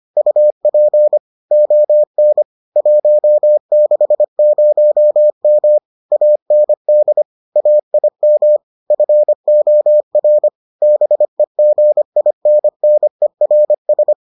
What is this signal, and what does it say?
Morse code, heard as a keyed tone, about 25 wpm, 600 Hz. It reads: UP ON 160M AND AIM FOR BEGINNERS